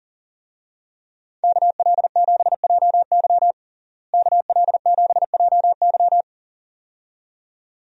Morse code, heard as a keyed tone, about 40 wpm, 700 Hz. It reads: KL7JY KL7JY
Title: DXCC-100 Call Signs - 5 Characters - Repeated 2x - 40wpm